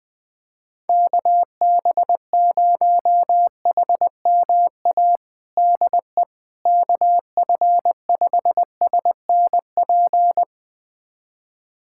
Morse code, heard as a keyed tone, about 20 wpm, 700 Hz. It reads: KB0HMA DE KF5SNP
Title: USA Callsign DE Callsign - 6 Characters - 20wpm